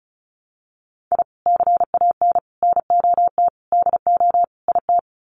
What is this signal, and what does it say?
Morse code, heard as a keyed tone, about 34 wpm, 700 Hz. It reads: I CAN NOT DO IT
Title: Sets of 5 Words Encore 35wpm